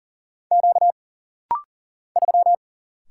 Morse code, heard as a keyed tone, about 39 wpm, 700 Hz. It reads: Q 3
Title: Single Letter-Number - Rapid-Fire 40wpm